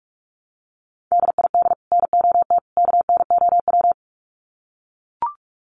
Morse code, heard as a keyed tone, about 44 wpm, 700 Hz. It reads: DID NOT KNOW
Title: Sets of 3 Words 45wpm